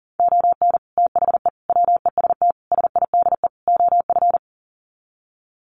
Morse code, heard as a keyed tone, about 40 wpm, 700 Hz. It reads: ON THE WEST SIDE OF
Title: Sets of 5 Words Encore2 40wpm